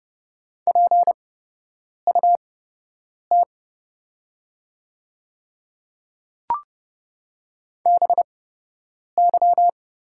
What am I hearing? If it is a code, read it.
Morse code, heard as a keyed tone, about 30 wpm, spaced as for 8 wpm, 700 Hz. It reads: PUT BY
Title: 8x Character Spacing - Top 200 Words - 30wpm